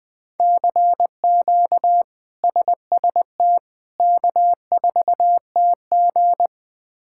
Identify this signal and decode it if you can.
Morse code, heard as a keyed tone, about 21 wpm, 700 Hz. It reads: CQ SST K4TG
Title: K1USN SST QSO Practice - 20wpm